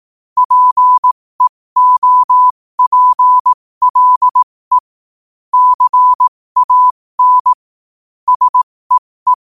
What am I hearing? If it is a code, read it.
Morse code, heard as a keyed tone, about 18 wpm, spaced as for 15 wpm, 1000 Hz. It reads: PEOPLE CAN SEE
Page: 15 WPM morse code quotes